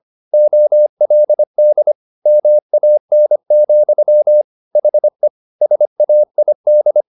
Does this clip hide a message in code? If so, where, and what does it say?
Yes: Morse code from 0.3 to 7.0 s, heard as a keyed tone, about 25 wpm, 600 Hz. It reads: OLD MAN, HE SAID